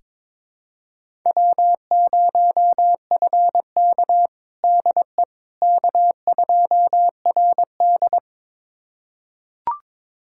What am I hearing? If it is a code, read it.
Morse code, heard as a keyed tone, about 22 wpm, 700 Hz. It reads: W0FK DE K2RD
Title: USA Callsign DE Callsign - 4 Characters - 22wpm